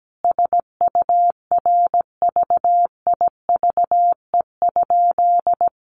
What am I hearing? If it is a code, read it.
Morse code, heard as a keyed tone, about 17 wpm, 700 Hz. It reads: SURVIVE?
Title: A Sound Of Thunder 17wpm